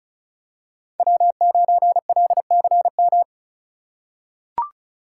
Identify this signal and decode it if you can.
Morse code, heard as a keyed tone, about 34 wpm, 700 Hz. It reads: W9LCM
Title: USA Call Signs - 5 Characters - 2x Character Spacing - 35wpm